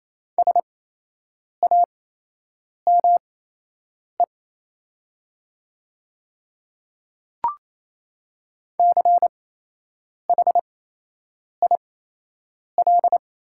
Morse code, heard as a keyed tone, about 27 wpm, spaced as for 8 wpm, 700 Hz. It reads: SAME CHIL
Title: 8x Character Spacing - Top 300 Words - 28wpm